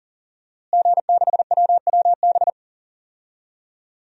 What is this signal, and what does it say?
Morse code, heard as a keyed tone, about 40 wpm, 700 Hz. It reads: G6WWB
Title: DXCC-100 Call Signs - 5 Characters - Repeated 2x - 40wpm